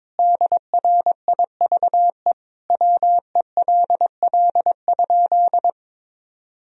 Morse code, heard as a keyed tone, about 22 wpm, 700 Hz. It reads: DRIVE WELL?
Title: Sentences from Top 1000 Words - Review Part 2 - 22wpm